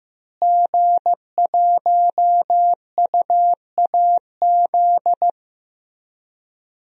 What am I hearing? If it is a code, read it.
Morse code, heard as a keyed tone, about 15 wpm, 700 Hz. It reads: G1UAZ
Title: DXCC-100 Call Signs - 5 Characters - Repeated 2x - 15wpm